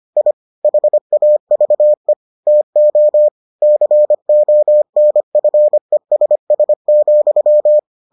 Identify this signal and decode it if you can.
Morse code, heard as a keyed tone, about 25 wpm, 600 Hz. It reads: I HAVE TO CONFESS,